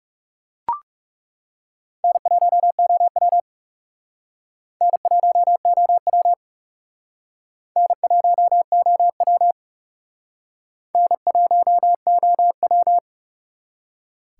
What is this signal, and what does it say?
Morse code, heard as a keyed tone, about 37 wpm, 700 Hz. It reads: N1OW N1OW N1OW N1OW
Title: Speed Racing - USA 1x2 Call Signs 30wpm